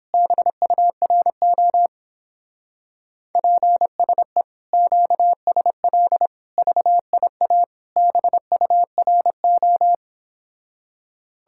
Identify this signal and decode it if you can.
Morse code, heard as a keyed tone, about 29 wpm, 700 Hz. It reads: BURO PSE QSL VIA BURO